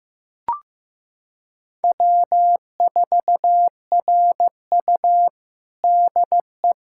Morse code, heard as a keyed tone, about 15 wpm, 700 Hz. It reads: W4RU DE
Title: USA Callsign DE Callsign - 4 Characters - 15wpm